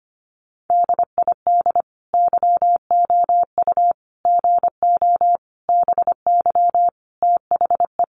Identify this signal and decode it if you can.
Morse code, heard as a keyed tone, about 25 wpm, 700 Hz. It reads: DID YOU GO BY THE
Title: Sentences from Top 1200 Words 25wpm